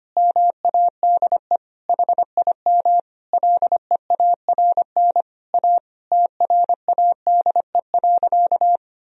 Morse code, heard as a keyed tone, about 25 wpm, 700 Hz. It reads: MADE HIM LEARN A TRADE.
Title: Sentences from Top 1000 Words - Review Part 4 - 25wpm